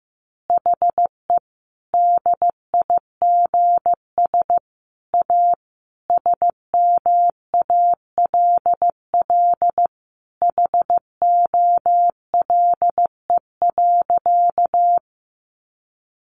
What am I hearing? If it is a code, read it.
Morse code, heard as a keyed tone, about 15 wpm, 700 Hz. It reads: HE DIGS A SMALL HOLE.
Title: Easy Phrases 15wpm